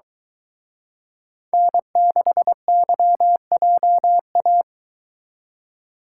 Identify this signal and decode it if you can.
Morse code, heard as a keyed tone, about 23 wpm, 700 Hz. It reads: N6YJA